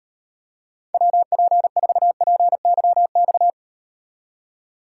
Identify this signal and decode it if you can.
Morse code, heard as a keyed tone, about 38 wpm, 700 Hz. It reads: WP4PYX